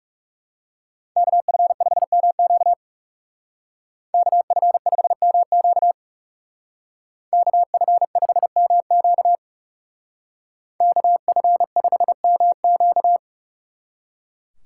Morse code, heard as a keyed tone, about 39 wpm, 700 Hz. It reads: KF5MQ KF5MQ KF5MQ KF5MQ